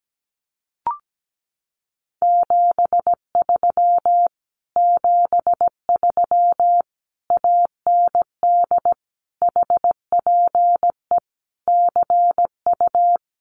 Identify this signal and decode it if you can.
Morse code, heard as a keyed tone, about 17 wpm, 700 Hz. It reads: 73 73 AND HPE CU
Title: Common QSO Phrases Encore 17wpm